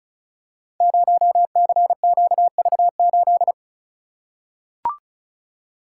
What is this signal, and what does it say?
Morse code, heard as a keyed tone, about 36 wpm, 700 Hz. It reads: 0CQV8
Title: Code Groups - 5 Letter:Numbers - 35wpm